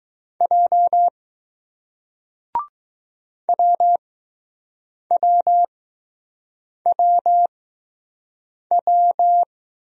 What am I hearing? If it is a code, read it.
Morse code, heard as a keyed tone, about 20 wpm, 700 Hz. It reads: J W W W W